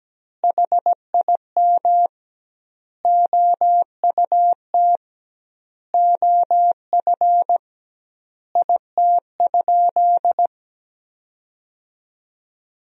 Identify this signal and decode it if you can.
Morse code, heard as a keyed tone, about 17 wpm, 700 Hz. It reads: HIM OUT OF IT?
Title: 2x Word-Spacing - Sentences from Top 300 Words - 17wpm